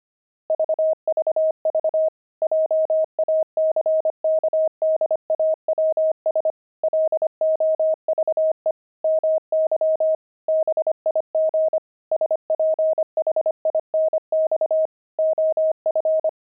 Morse code, heard as a keyed tone, about 25 wpm, 630 Hz. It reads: VVV JACKDAWS LOVE MY BIG SPHINX OF
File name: pangram-fs-25-cs-25-jackdaws